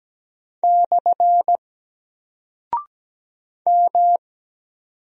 Morse code, heard as a keyed tone, about 17 wpm, 700 Hz. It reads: / M
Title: Single Character (Letter:Number:Punctuation:Prosign) 17wpm